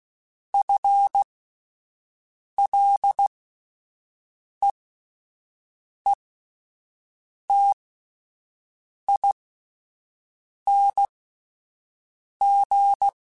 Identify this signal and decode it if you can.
Morse code, heard as a keyed tone, about 16 wpm, spaced as for 5 wpm, 800 Hz. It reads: FLEETING
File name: Combinations / Combinations-mp3s / Combinations@5